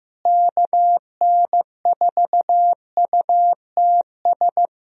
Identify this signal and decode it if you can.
Morse code, heard as a keyed tone, about 15 wpm, 700 Hz. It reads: KN4UTS
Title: USA Call Signs - 6 Characters - Repeated 3x - 15wpm